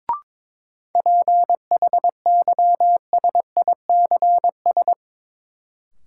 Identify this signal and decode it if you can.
Morse code, heard as a keyed tone, about 22 wpm, 700 Hz. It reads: PHYSICS
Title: Sentences from Top 1000 Words - Review Part 5 - New Word Drill - 22wpm